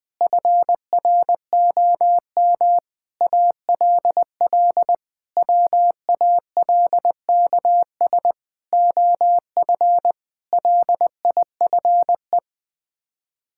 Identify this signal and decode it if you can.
Morse code, heard as a keyed tone, about 20 wpm, 700 Hz. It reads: FROM ALL WALKS OF LIFE
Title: Sets of 6 Words - 20wpm